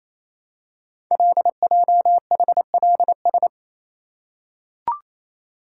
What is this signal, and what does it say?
Morse code, heard as a keyed tone, about 28 wpm, 700 Hz. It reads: LJHLS
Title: Code Groups - 5 Letters - 28wpm